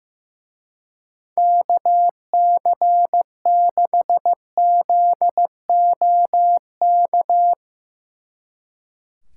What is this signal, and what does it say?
Morse code, heard as a keyed tone, about 16 wpm, 700 Hz. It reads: KC6ZOK
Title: Speed Racing - USA 2x3 Call Signs 15wpm